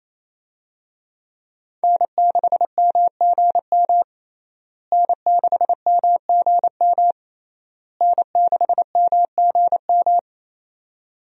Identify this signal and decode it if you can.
Morse code, heard as a keyed tone, about 28 wpm, 700 Hz. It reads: N6MGM N6MGM N6MGM
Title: USA Call Signs - 5 Characters - Repeated 3x - 28wpm